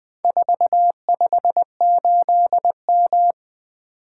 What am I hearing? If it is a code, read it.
Morse code, heard as a keyed tone, about 20 wpm, 700 Hz. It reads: 458M
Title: Four Letter-Number Combinations - 20wpm